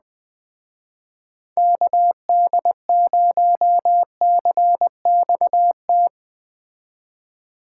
Morse code, heard as a keyed tone, about 20 wpm, 700 Hz. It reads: KD0CXT
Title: Speed Racing - USA 2x3 Call Signs 17wpm